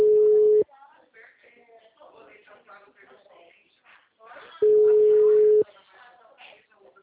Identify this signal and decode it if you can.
Ringback tone (Europe, 425 Hz), with speech.